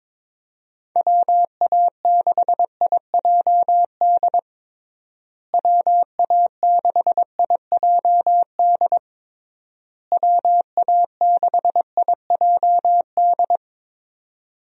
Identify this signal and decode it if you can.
Morse code, heard as a keyed tone, about 22 wpm, 700 Hz. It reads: WA6IJD WA6IJD WA6IJD
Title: USA Call Signs - 6 Characters - Repeated 3x - 22wpm